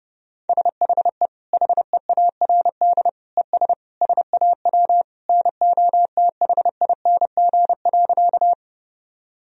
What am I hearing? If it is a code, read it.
Morse code, heard as a keyed tone, about 30 wpm, 700 Hz. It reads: SHE HEARD ES SAW NOTHING.
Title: Sentences from Top 500 Words and Abbreviations 30wpm